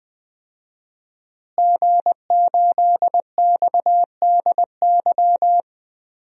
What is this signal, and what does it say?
Morse code, heard as a keyed tone, about 20 wpm, 700 Hz. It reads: G8XDY
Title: DXCC-100 Call Signs - 5 Characters - Repeated 3x - 20wpm